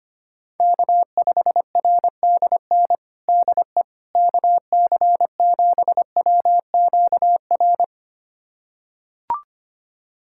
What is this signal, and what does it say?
Morse code, heard as a keyed tone, about 25 wpm, 700 Hz. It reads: K5RDN DE KC7WQR
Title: USA Callsign DE Callsign 25wpm